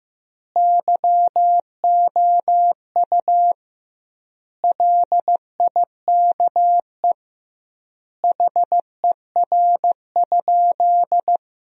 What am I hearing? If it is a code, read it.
Morse code, heard as a keyed tone, about 16 wpm, 700 Hz. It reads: YOU LIKE HER?
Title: 2x Word-Spacing - Sentences from Top 100 Words - 15wpm